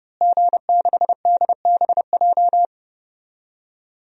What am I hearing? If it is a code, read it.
Morse code, heard as a keyed tone, about 30 wpm, 700 Hz. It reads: G6DBJ
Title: DXCC-100 Call Signs - 5 Characters - Repeated 3x - 30wpm